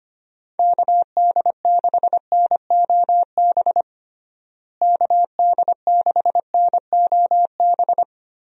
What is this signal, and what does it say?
Morse code, heard as a keyed tone, about 25 wpm, 700 Hz. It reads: KD6NOB KD6NOB
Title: USA Call Signs - 6 Characters - Repeated 3x - 25wpm